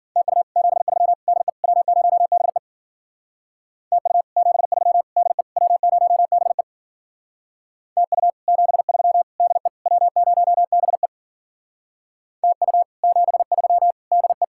Morse code, heard as a keyed tone, about 51 wpm, 700 Hz. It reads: TU 73 DE W0BE TU 73 DE W0BE TU 73 DE W0BE TU 73 DE